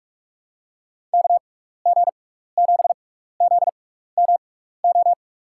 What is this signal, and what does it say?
Morse code, heard as a keyed tone, about 44 wpm, spaced as for 15 wpm, 700 Hz. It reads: KG7ZMO
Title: USA Call Signs - 6 Characters - 6x Character Spacing - 45wpm